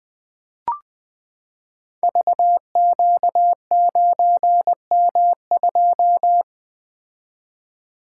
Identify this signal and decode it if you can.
Morse code, heard as a keyed tone, about 20 wpm, 700 Hz. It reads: VQ9M2
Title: Code Groups - 5 Letter:Numbers - 20wpm